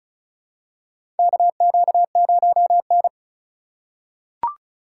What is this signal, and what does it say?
Morse code, heard as a keyed tone, about 36 wpm, 700 Hz. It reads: KQ0N
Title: USA Call Signs - 4 Characters - 4x Character Spacing - 35wpm